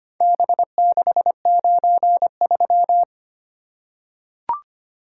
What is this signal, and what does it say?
Morse code, heard as a keyed tone, about 25 wpm, 700 Hz. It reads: B693
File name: practice / 25wpm / Four Letter-Number Combinations - 25wpm